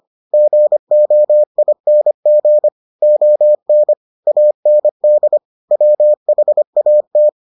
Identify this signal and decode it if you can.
Morse code, heard as a keyed tone, about 25 wpm, 600 Hz. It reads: GOING ON AND WHAT